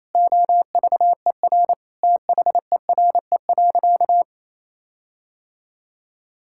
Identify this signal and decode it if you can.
Morse code, heard as a keyed tone, about 28 wpm, 700 Hz. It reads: OVER THERE.
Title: Sentences from Top 1600 Words 28wpm